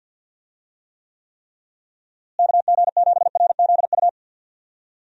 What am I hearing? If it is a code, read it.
Morse code, heard as a keyed tone, about 50 wpm, 700 Hz. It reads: KG7RZU
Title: USA Call Signs - 6 Characters - Repeated 3x - 50wpm